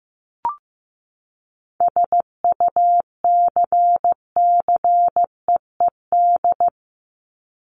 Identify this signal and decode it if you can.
Morse code, heard as a keyed tone, about 15 wpm, 700 Hz. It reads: SUCCEED
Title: Top 1501-1750 Words 15wpm